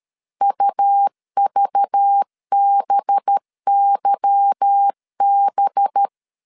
Morse code, heard as a keyed tone, about 13 wpm, 800 Hz. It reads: UVBYB
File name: CW / DET / 1 9 1500C S B 12wpm 3819